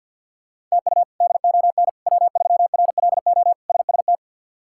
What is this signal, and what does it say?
Morse code, heard as a keyed tone, about 50 wpm, 700 Hz. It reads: TU DON W3RLO SST
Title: K1USN SST QSO Practice - 50wpm